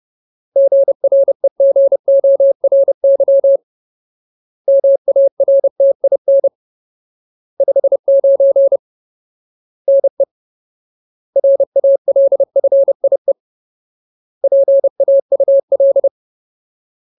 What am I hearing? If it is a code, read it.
Morse code, heard as a keyed tone, about 30 wpm, 550 Hz. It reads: GREGORY MARTIN 59 NE RALFIE PAUL
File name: Ultimate Word List / 30WPM Ultimate